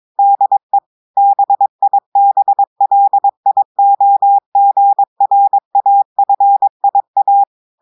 Morse code, heard as a keyed tone, about 22 wpm, 800 Hz. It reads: DE BIBLIOGRAFIA
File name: audios / portugais / portugais texte en clair 1 vitesse 22 mots minutes